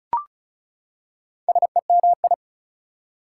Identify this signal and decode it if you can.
Morse code, heard as a keyed tone, about 36 wpm, 700 Hz. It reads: SEMI
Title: Prefixes and Suffixes 35wpm